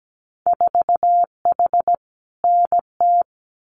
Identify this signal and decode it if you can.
Morse code, heard as a keyed tone, about 17 wpm, 700 Hz. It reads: 4H NT
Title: Winter Field Day Exchanges Encore 17wpm